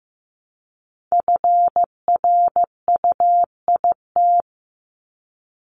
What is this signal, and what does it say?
Morse code, heard as a keyed tone, about 15 wpm, 700 Hz. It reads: FRUIT
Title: Top 1251-1500 Words 15wpm